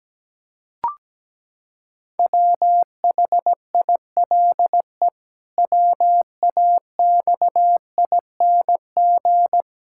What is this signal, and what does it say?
Morse code, heard as a keyed tone, about 17 wpm, 700 Hz. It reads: WHILE WAXING
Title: Pangram Challenge v2 - 17wpm